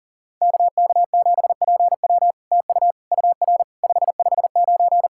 Morse code, heard as a keyed tone, about 40 wpm, 700 Hz. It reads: KK7PW TU UR 559
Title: SOTA QSO Practice 40wpm